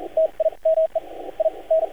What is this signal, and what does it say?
Morse code, heard as a keyed tone, about 42 wpm, spaced as for 32 wpm, 640 Hz, with radio noise. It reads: TIME IN
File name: archived / CRSnet